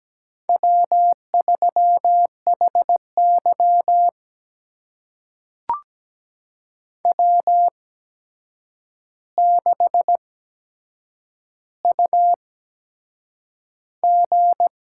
Morse code, heard as a keyed tone, about 17 wpm, spaced as for 7 wpm, 700 Hz. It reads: W3HY W6UG
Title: USA Call Signs - 4 Characters - 8x Character Spacing - 17wpm